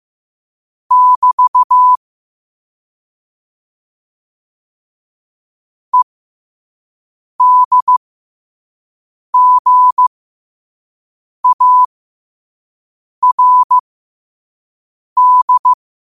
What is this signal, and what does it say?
Morse code, heard as a keyed tone, about 15 wpm, spaced as for 5 wpm, 1000 Hz. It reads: <BT> EDGARD
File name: qotdfiles / 5wpm